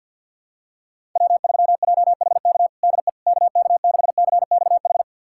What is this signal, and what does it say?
Morse code, heard as a keyed tone, about 51 wpm, 700 Hz. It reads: W3JHK DE KK6CXH
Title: USA Callsign DE Callsign 50wpm